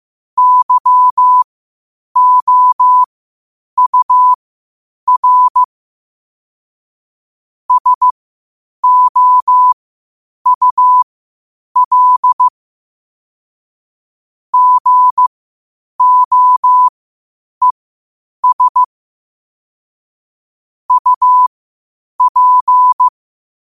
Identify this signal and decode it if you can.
Morse code, heard as a keyed tone, about 15 wpm, spaced as for 8 wpm, 1000 Hz. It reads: YOUR SOUL GOES UP
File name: qotdfiles / 8wpm